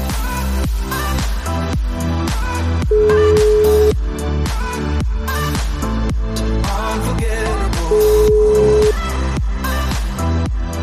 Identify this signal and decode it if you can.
Ringback tone (Europe, 425 Hz), with noise.